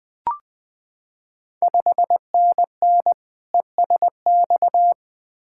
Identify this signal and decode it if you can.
Morse code, heard as a keyed tone, about 20 wpm, 690 Hz. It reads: 5NN ESX
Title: Ontario QSO Party Exchange 20wpm